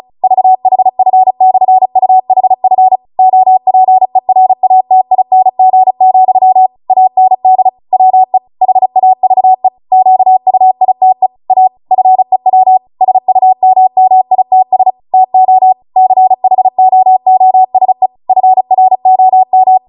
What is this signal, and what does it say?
Morse code, heard as a keyed tone, about 36 wpm, 750 Hz. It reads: VHF/UHF OPERATING, AND WE HAVE QUITE A FEW SUMMITS TO CHOOSE FROM